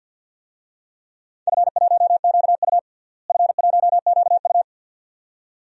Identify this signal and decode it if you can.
Morse code, heard as a keyed tone, about 48 wpm, 700 Hz. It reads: F1QU F1QU
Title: DXCC-100 Call Signs - 4 Characters - Repeated 2x - 50wpm